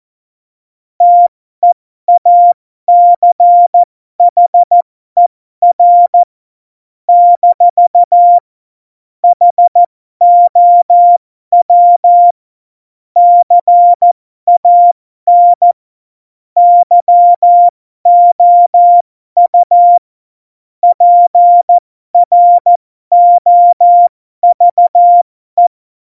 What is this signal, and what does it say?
Morse code, heard as a keyed tone, about 14 wpm, spaced as for 12 wpm, 700 Hz. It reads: TEACHER - HOW CAN YOU PROVE
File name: Morsekurs / Witze